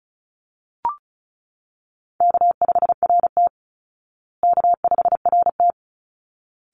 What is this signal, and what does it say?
Morse code, heard as a keyed tone, about 34 wpm, 700 Hz. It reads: K5RT K5RT
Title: Call Signs Repeated 3x - 35wpm